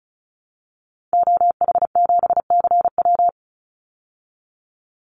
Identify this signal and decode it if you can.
Morse code, heard as a keyed tone, about 34 wpm, 700 Hz. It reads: OH7CW
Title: Call Signs Repeated 2x - 35wpm